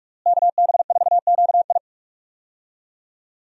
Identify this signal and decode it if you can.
Morse code, heard as a keyed tone, about 44 wpm, 700 Hz. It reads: KB4QI
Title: USA Call Signs - 5 Characters - Repeated 3x - 45wpm